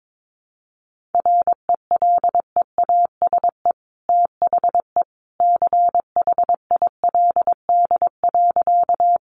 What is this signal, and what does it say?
Morse code, heard as a keyed tone, about 22 wpm, 700 Hz. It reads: RELEASE THE CHILD.
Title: Sentences from Top 1100 Words 22wpm